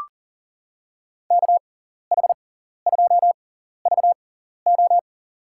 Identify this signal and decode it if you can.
Morse code, heard as a keyed tone, about 40 wpm, spaced as for 14 wpm, 700 Hz. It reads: KH2VO
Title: USA Call Signs - 6x Character Spacing - 40wpm